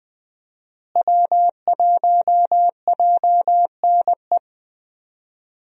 Morse code, heard as a keyed tone, about 20 wpm, 700 Hz. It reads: W1JNE
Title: USA Call Signs - 5 Characters - Repeated 3x - 20wpm